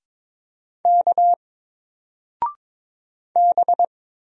Morse code, heard as a keyed tone, about 22 wpm, 700 Hz. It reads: K B
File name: Single-Letter-Number-22wpm